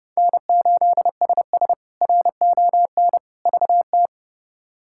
Morse code, heard as a keyed tone, about 32 wpm, 700 Hz. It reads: N8SS RON VT
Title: K1USN SST QSO Practice - 30wpm